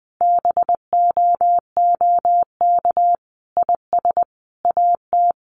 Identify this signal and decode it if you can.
Morse code, heard as a keyed tone, about 19 wpm, 700 Hz. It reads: BOOK IS AT
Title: Sentences from Top 1200 Words 20wpm